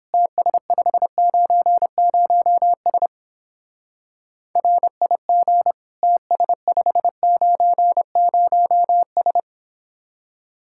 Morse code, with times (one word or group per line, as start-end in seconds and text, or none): 0.1-3.1 TS590S
4.6-5.7 RIG
6.0-9.4 TS590S